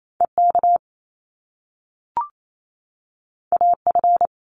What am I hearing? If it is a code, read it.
Morse code, heard as a keyed tone, about 28 wpm, 700 Hz. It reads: EK AF
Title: Top Two Letter Combinations in Words 28wpm